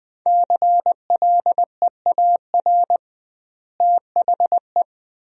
Morse code, heard as a keyed tone, about 20 wpm, 700 Hz. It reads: CLEAR THE
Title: 2x Word-Spacing - Sentences from Top 500 Words - 20wpm